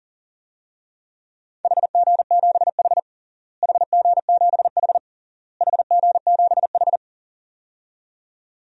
Morse code, heard as a keyed tone, about 40 wpm, 700 Hz. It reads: HG7H HG7H HG7H
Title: DXCC-100 Call Signs - 4 Characters - Repeated 3x - 40wpm